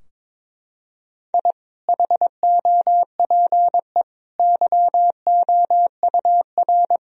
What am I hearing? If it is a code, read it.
Morse code, heard as a keyed tone, about 22 wpm, 700 Hz. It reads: I HOPE YOUR
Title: Sentences from Top 1000 Words - Review Part 3 - 22wpm